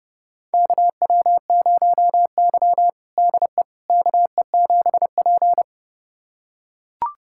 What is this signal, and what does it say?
Morse code, heard as a keyed tone, about 32 wpm, 700 Hz. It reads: KW0Y DE KE7P
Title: USA Callsign DE Callsign - 4 Characters - 30wpm